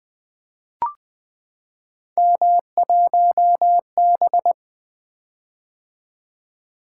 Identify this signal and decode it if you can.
Morse code, heard as a keyed tone, about 20 wpm, 700 Hz. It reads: M1B
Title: Three Letter-Number Combinations - 20wpm